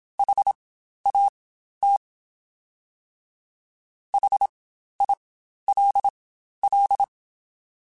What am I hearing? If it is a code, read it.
Morse code, heard as a keyed tone, about 27 wpm, spaced as for 12 wpm, 800 Hz. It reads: HAT HILL
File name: Spaced Alphas